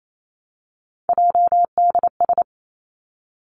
Morse code, heard as a keyed tone, about 28 wpm, 700 Hz. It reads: JDS
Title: Three Challenging Letter-Number Combinations 28wpm